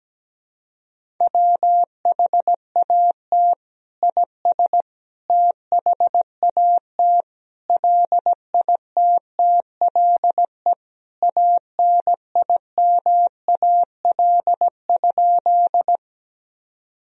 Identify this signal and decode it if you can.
Morse code, heard as a keyed tone, about 17 wpm, 700 Hz. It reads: WHAT IS THAT LITTLE ANIMAL?
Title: Sentences from Top 1000 Words - Review Part 1 - 17wpm